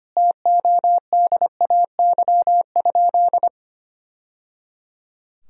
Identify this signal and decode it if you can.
Morse code, heard as a keyed tone, about 25 wpm, 700 Hz. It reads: TODAY?
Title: Sentences from Top 1000 Words - Review Part 4 - 25wpm